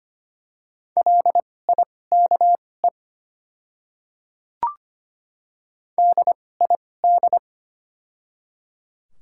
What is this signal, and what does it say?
Morse code, heard as a keyed tone, about 25 wpm, spaced as for 18 wpm, 690 Hz. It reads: LIKE DID
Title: 2x Character Spacing - Top 100 Words - 25wpm